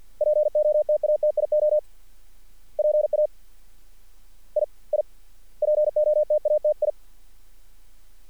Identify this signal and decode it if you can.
Morse code, heard as a keyed tone, about 50 wpm, 600 Hz. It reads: POTATIO PA I I POTATI
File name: archived / CRSnet